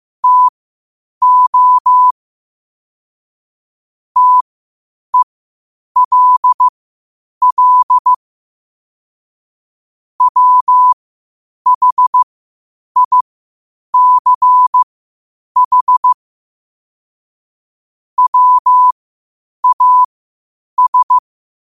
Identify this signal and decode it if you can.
Morse code, heard as a keyed tone, about 15 wpm, spaced as for 8 wpm, 1000 Hz. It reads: TO TELL WHICH WAS
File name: qotdfiles / 8wpm